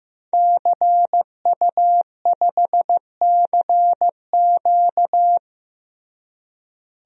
Morse code, heard as a keyed tone, about 15 wpm, 700 Hz. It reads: CU5CQ
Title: DXCC-100 Call Signs - 5 Characters - Repeated 3x - 15wpm